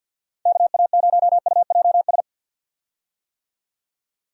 Morse code, heard as a keyed tone, about 50 wpm, 700 Hz. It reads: KA0UJS
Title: USA Call Signs - 6 Characters - 50wpm